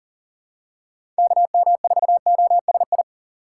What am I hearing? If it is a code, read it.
Morse code, heard as a keyed tone, about 40 wpm, 700 Hz. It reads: KM4OSI
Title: USA Call Signs - 6 Characters - 2x Character Spacing - 40wpm